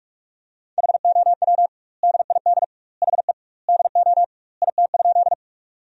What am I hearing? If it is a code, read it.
Morse code, heard as a keyed tone, about 46 wpm, 700 Hz. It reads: HOW DID HE DO IT?
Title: 2x Word-Spacing - Sentences from Top 100 Words - 45wpm